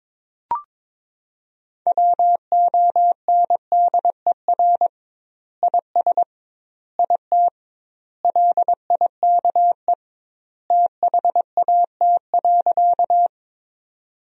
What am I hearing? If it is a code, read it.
Morse code, heard as a keyed tone, about 22 wpm, 700 Hz. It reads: WONDER IS IT LIKE THAT.
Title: 2x Word-Spacing - Sentences from Top 500 Words - 22wpm